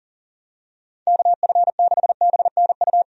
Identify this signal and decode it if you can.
Morse code, heard as a keyed tone, about 40 wpm, 700 Hz. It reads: KF6BNU
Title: USA Call Signs - 6 Characters - 40wpm